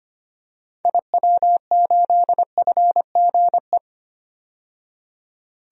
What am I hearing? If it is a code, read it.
Morse code, heard as a keyed tone, about 25 wpm, 700 Hz. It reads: IW8FGE
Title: DXCC-100 Call Signs - 6 Characters - 25wpm